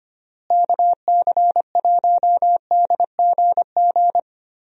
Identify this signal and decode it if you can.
Morse code, heard as a keyed tone, about 25 wpm, 700 Hz. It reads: KC1DGG
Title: USA Call Signs - 6 Characters - Repeated 2x - 25wpm